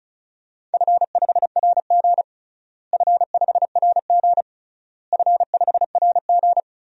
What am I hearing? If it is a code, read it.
Morse code, heard as a keyed tone, about 36 wpm, 700 Hz. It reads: F5RG F5RG F5RG
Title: DXCC-100 Call Signs - 4 Characters - Repeated 3x - 35wpm